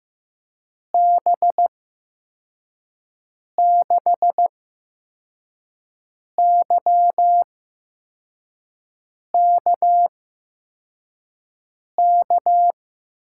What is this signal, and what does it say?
Morse code, heard as a keyed tone, about 15 wpm, spaced as for 4 wpm, 700 Hz. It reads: B6YKK
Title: USA Call Signs - 8x Character Spacing - 15wpm